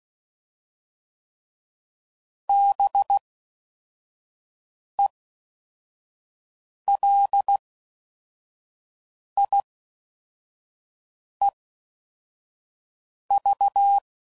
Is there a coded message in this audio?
Yes: Morse code from 2.5 to 14.0 s, heard as a keyed tone, about 16 wpm, spaced as for 4 wpm, 800 Hz. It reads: BELIEV